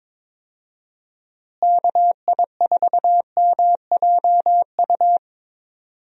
Morse code, heard as a keyed tone, about 22 wpm, 700 Hz. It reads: KI4MJU
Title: USA Call Signs - 6 Characters - Repeated 3x - 22wpm